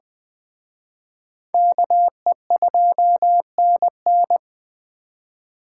Morse code, heard as a keyed tone, about 24 wpm, 700 Hz. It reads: KE2NN
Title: Speed Racing - USA Call Signs 20wpm